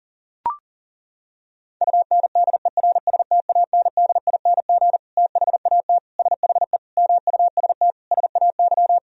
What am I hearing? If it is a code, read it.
Morse code, heard as a keyed tone, about 40 wpm, 700 Hz. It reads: UNDERSTANDING THAT SHE MUST SAY
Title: Sentences from Top 1000 Words - Review Part 3 - 40wpm